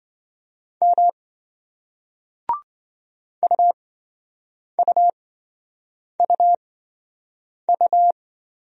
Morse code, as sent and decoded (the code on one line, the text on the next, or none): -- / ..- ..- ..- ..-
M UUUU